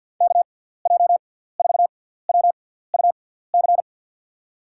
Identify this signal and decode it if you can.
Morse code, heard as a keyed tone, about 50 wpm, spaced as for 17 wpm, 700 Hz. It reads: KJ4WUC
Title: USA Call Signs - 6 Characters - 6x Character Spacing - 50wpm